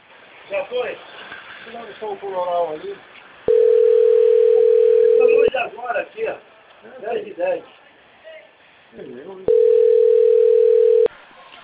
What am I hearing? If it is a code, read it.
Ringback tone (North America), 440 + 480 Hz, with speech.
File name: rg-1600-unknown- - - .44888